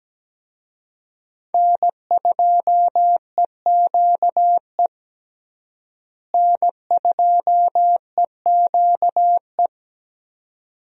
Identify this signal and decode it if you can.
Morse code, heard as a keyed tone, about 17 wpm, 700 Hz. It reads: N2EQE N2EQE
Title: USA Call Signs - 5 Characters - Repeated 2x - 17wpm